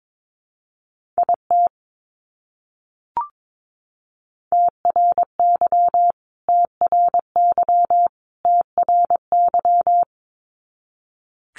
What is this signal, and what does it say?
Morse code, heard as a keyed tone, about 22 wpm, 700 Hz. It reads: IT TRY TRY TRY
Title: Top 200 Words Repeated 22wpm